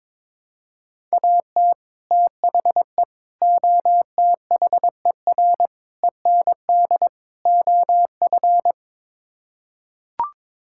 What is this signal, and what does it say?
Morse code, heard as a keyed tone, about 21 wpm, 700 Hz. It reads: AT THE OTHER END OF
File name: practice / 22wpm / Build A Phrase - 5-Word Phrases - 22wpm